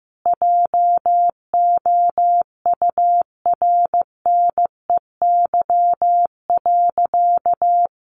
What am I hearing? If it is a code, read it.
Morse code, heard as a keyed tone, about 15 wpm, 700 Hz. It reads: JOURNEY.
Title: A Sound Of Thunder 15wpm